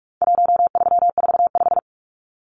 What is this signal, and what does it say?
Morse code, heard as a keyed tone, about 44 wpm, 700 Hz. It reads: 1345
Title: Numbers - 4-Digit 45wpm